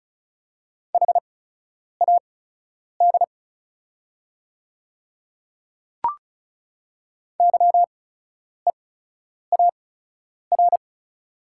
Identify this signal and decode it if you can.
Morse code, heard as a keyed tone, about 36 wpm, spaced as for 10 wpm, 690 Hz. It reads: HAD YEAR
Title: 8x Character Spacing - Top 300 Words - 35wpm